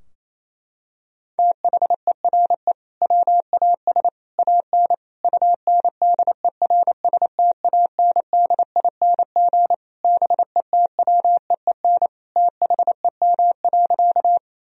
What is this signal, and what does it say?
Morse code, heard as a keyed tone, about 27 wpm, 700 Hz. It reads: THERE WAS AN UNDERSTANDING BETWEEN THEM.
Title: Sentences from Top 1000 Words - Review Part 3 - 28wpm